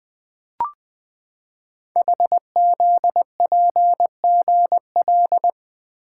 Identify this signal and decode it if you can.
Morse code, heard as a keyed tone, about 20 wpm, 700 Hz. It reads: HZPGL
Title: Code Groups - 5 Letters - 20wpm